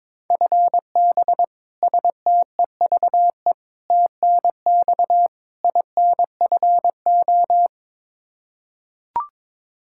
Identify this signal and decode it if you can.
Morse code, heard as a keyed tone, about 22 wpm, 700 Hz. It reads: FB STEVE TNX INFO
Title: Practice QSO - Exchange 3 - More Info - 22wpm